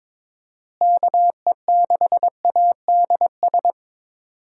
Morse code, heard as a keyed tone, about 22 wpm, 700 Hz. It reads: KE6ADS
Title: USA Call Signs - 6 Characters - Repeated 2x - 22wpm